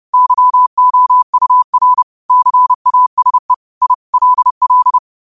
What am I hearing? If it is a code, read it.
Morse code, heard as a keyed tone, about 30 wpm, 1000 Hz. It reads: YOUR CASE I LL